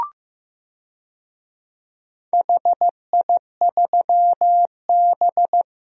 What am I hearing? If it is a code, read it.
Morse code, heard as a keyed tone, about 15 wpm, 700 Hz. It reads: HI3B
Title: DXCC-100 Call Signs - 4 Characters - Repeated 2x - 15wpm